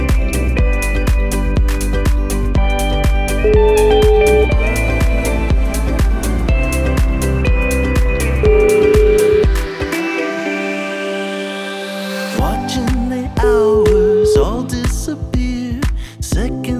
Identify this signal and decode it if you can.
Ringback tone (Europe, 425 Hz), with noise.